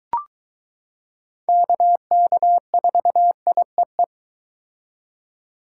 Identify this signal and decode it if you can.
Morse code, heard as a keyed tone, about 22 wpm, 700 Hz. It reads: KK4IEE